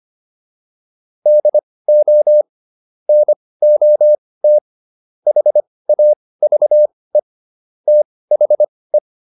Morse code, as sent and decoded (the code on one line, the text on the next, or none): -.. --- / -. --- - / .... .- ...- . / - .... .
DO NOT HAVE THE